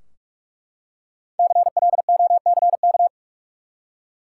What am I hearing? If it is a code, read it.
Morse code, heard as a keyed tone, about 41 wpm, 700 Hz. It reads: CLOCK
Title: Five Letter Words - 45wpm